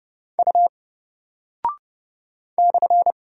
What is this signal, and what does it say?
Morse code, heard as a keyed tone, about 30 wpm, 700 Hz. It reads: U /
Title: Single Character (Letter:Number:Punctuation:Prosign) Alt 30wpm